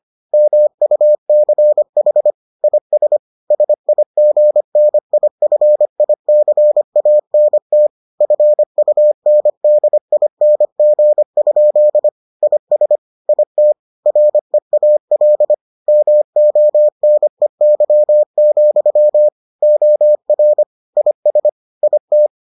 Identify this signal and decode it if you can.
Morse code, heard as a keyed tone, about 25 wpm, 600 Hz. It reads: MUCH IS SIGNIFICANT FUNDING? IS IT REAL MONEY, OR IS IT